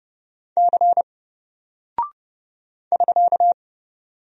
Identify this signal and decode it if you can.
Morse code, heard as a keyed tone, about 30 wpm, 700 Hz. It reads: C <SK>
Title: Single Character (Letter:Number:Punctuation:Prosign) Alt 30wpm